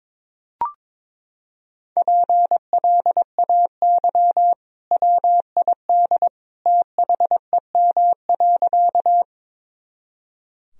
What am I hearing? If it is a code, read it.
Morse code, heard as a keyed tone, about 21 wpm, 700 Hz. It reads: PLAY WID THEM.
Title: Sentences from Top 200 Words and Abbreviations 22wpm